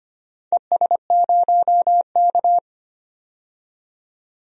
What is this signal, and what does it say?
Morse code, heard as a keyed tone, about 25 wpm, 700 Hz. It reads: ES0K
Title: DXCC-100 Call Signs - 4 Characters - 25wpm